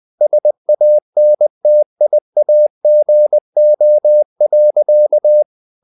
Morse code, heard as a keyed tone, about 20 wpm, 600 Hz. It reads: SANTIAGO.
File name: the old man and the sea part 2 20wpm